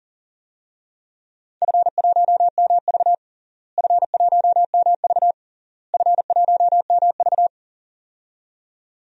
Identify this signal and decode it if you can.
Morse code, heard as a keyed tone, about 40 wpm, 700 Hz. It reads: F1MV F1MV F1MV
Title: DXCC-100 Call Signs - 4 Characters - Repeated 3x - 40wpm